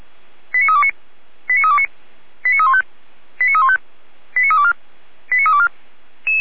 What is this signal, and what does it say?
Five-tone selective call (ZVEI), heard as quick runs of five tones, tones of about 70 ms. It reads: 89329 89329 89315 89315 89324 89324